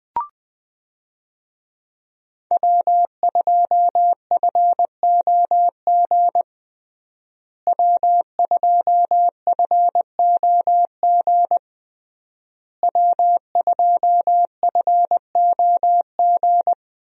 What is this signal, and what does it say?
Morse code, heard as a keyed tone, about 20 wpm, 700 Hz. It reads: W2FOG W2FOG W2FOG
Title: USA Call Signs - 5 Characters - Repeated 3x - 20wpm